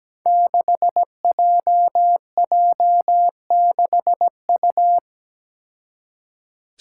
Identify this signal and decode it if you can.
Morse code, heard as a keyed tone, about 17 wpm, 700 Hz. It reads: 6JJ6U
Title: Five Challenging Letter-Number Combinations 17wpm